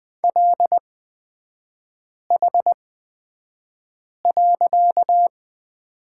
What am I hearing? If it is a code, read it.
Morse code, heard as a keyed tone, about 20 wpm, spaced as for 5 wpm, 700 Hz. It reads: LH.
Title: 255 Lesson 41e - Period - Three Characters 20wpm